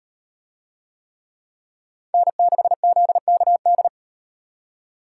Morse code, heard as a keyed tone, about 38 wpm, 700 Hz. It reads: N6ZKD